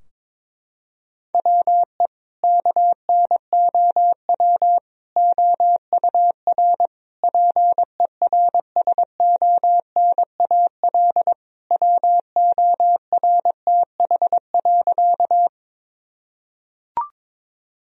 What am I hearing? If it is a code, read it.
Morse code, heard as a keyed tone, about 22 wpm, 700 Hz. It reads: WE KNOW OUR PERSONAL WORTH.
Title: Sentences from Top 1000 Words - Review Part 4 - 22wpm